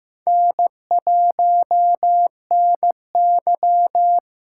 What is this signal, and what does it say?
Morse code, heard as a keyed tone, about 15 wpm, 700 Hz. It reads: N1NY
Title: USA Callsign DE Callsign - 4 Characters - 15wpm